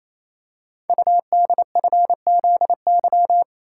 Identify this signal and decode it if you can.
Morse code, heard as a keyed tone, about 29 wpm, 700 Hz. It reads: UDFZY
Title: Code Groups - 5 Letters - 28wpm